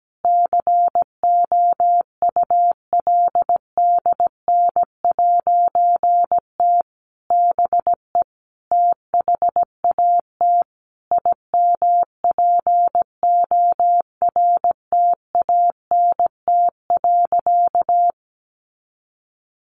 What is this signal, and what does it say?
Morse code, heard as a keyed tone, about 17 wpm, 700 Hz. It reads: COULDN'T BE THAT IMPORTANT.
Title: A Sound Of Thunder 17wpm